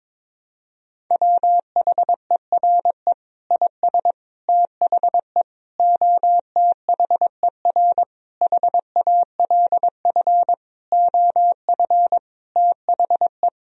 Morse code, heard as a keyed tone, about 22 wpm, 700 Hz. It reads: WHERE IS THE OTHER HALF OF THE
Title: Sentences from Top 1000 Words - Review Part 2 - 22wpm